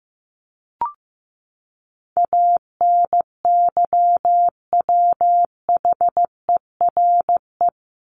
Morse code, heard as a keyed tone, about 15 wpm, 700 Hz. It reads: ANYWHERE
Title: Top 1251-1500 Words 15wpm